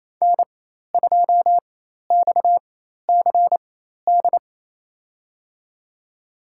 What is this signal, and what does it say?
Morse code, heard as a keyed tone, about 28 wpm, spaced as for 13 wpm, 700 Hz. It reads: N2XCD
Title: USA Call Signs - 4x Character Spacing - 28wpm